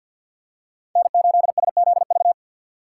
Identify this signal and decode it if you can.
Morse code, heard as a keyed tone, about 46 wpm, 700 Hz. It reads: N8SZV
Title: USA Call Signs - 5 Characters - 4x Character Spacing - 50wpm